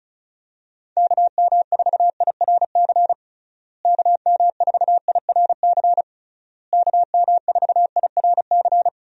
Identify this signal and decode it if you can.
Morse code, heard as a keyed tone, about 34 wpm, 700 Hz. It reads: KM4IRC KM4IRC KM4IRC
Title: USA Call Signs - 6 Characters - Repeated 3x - 35wpm